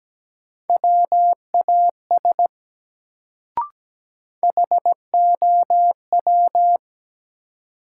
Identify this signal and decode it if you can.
Morse code, heard as a keyed tone, about 17 wpm, 700 Hz. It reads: WAS HOW
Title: Sentences from Top 100 Words - New Word Drill - 17wpm